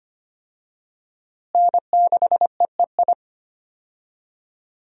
Morse code, heard as a keyed tone, about 25 wpm, 700 Hz. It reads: N6EEI
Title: USA Call Signs - 5 Characters - Repeated 2x - 25wpm